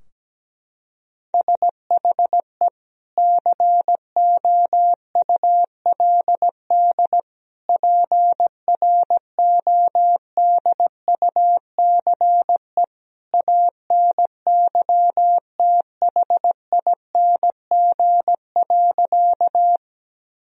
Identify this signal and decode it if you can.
Morse code, heard as a keyed tone, about 17 wpm, 700 Hz. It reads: SHE COULD PRODUCE ANYTHING.
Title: Sentences from Top 1000 Words - Review Part 3 - 17wpm